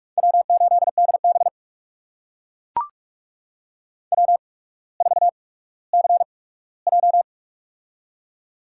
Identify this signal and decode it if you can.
Morse code, heard as a keyed tone, about 44 wpm, spaced as for 19 wpm, 690 Hz. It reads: W8DB W4CJ
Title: USA Call Signs - 4 Characters - 8x Character Spacing - 45wpm